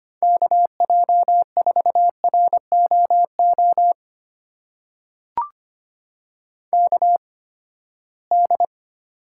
Morse code, heard as a keyed tone, about 25 wpm, spaced as for 17 wpm, 700 Hz. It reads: KJ4ROO KD
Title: USA Call Signs - 6 Characters - 8x Character Spacing - 25wpm